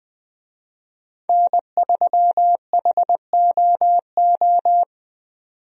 Morse code, heard as a keyed tone, about 20 wpm, 700 Hz. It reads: N3HOO
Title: USA Call Signs - 5 Characters - 20wpm